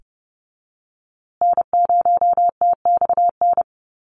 Morse code, heard as a keyed tone, about 30 wpm, 700 Hz. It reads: N0TXN
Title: USA Call Signs Repeated 2x 30wpm